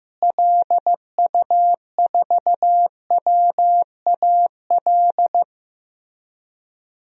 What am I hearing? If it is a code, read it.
Morse code, heard as a keyed tone, about 15 wpm, 700 Hz. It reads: LU4WAL
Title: DXCC-100 Call Signs - 6 Characters - Repeated 3x - 15wpm